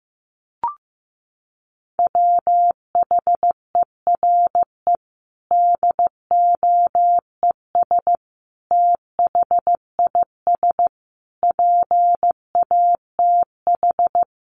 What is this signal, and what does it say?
Morse code, heard as a keyed tone, about 15 wpm, 700 Hz. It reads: WHERE DOES THIS PATH
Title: Sentences from Top 1400 Words 15wpm